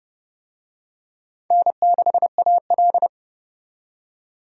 Morse code, heard as a keyed tone, about 30 wpm, 700 Hz. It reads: N6AL
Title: USA Call Signs - 4 Characters - 30wpm